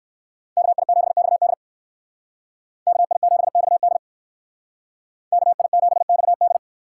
Morse code, heard as a keyed tone, about 57 wpm, 700 Hz. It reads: KI7XD KI7XD KI7XD